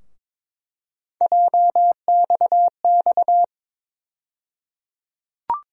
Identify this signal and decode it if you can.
Morse code, heard as a keyed tone, about 22 wpm, 700 Hz. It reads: JXX
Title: USA Call Sign Postfixes - 3 Letters 22wpm